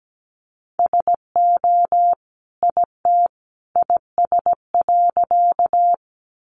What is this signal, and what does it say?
Morse code, heard as a keyed tone, about 17 wpm, 700 Hz. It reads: SO IT IS.
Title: Sentences from Top 100 Words 17wpm